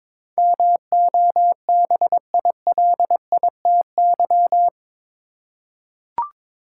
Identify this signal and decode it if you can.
Morse code, heard as a keyed tone, about 21 wpm, 700 Hz. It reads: MOBILITY
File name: practice / 22wpm / Eight Letter Words - 22wpm